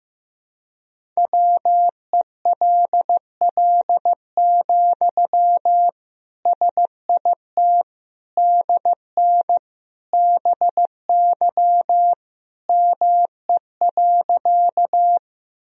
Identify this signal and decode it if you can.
Morse code, heard as a keyed tone, about 15 wpm, 700 Hz. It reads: WELL, SIT DN BY ME.
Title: Sentences from Top 500 Words and Abbreviations 15wpm